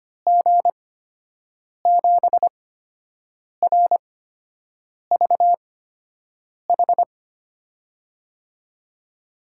Morse code, heard as a keyed tone, about 25 wpm, spaced as for 7 wpm, 700 Hz. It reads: G7RVH
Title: USA Call Signs - 6 Characters - 8x Character Spacing - 25wpm